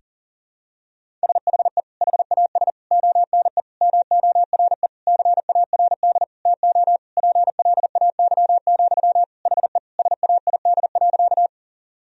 Morse code, heard as a keyed tone, about 40 wpm, 700 Hz. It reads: SHE HAS ONE MORE CARD TO PLAY, HE SAID.
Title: Sentences from Top 1000 Words - Review Part 4 - 40wpm